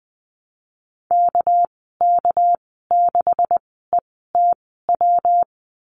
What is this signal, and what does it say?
Morse code, heard as a keyed tone, about 20 wpm, spaced as for 15 wpm, 700 Hz. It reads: KK6ETW
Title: 200 Lesson 32g - Letter k - Call Signs 20wpm